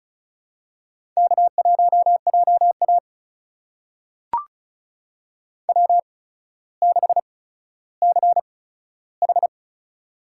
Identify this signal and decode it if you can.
Morse code, heard as a keyed tone, about 36 wpm, spaced as for 15 wpm, 700 Hz. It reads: K1JA W6CH
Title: USA Call Signs - 4 Characters - 8x Character Spacing - 35wpm